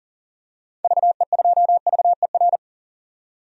Morse code, heard as a keyed tone, about 37 wpm, 700 Hz. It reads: VE2VER
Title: DXCC-100 Call Signs - 6 Characters - 40wpm